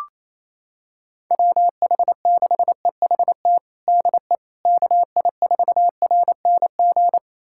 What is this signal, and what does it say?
Morse code, heard as a keyed tone, about 28 wpm, 700 Hz. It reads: WH6EHT DE KI4RNG
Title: USA Callsign DE Callsign 28wpm